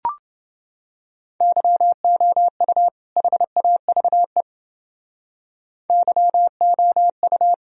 Morse code, heard as a keyed tone, about 29 wpm, 700 Hz. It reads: YOU HAVE YOU